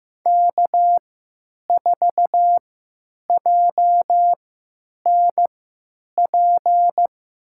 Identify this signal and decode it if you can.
Morse code, heard as a keyed tone, about 16 wpm, spaced as for 9 wpm, 700 Hz. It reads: K4JNP
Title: USA Call Signs - 3x Character Spacing - 15wpm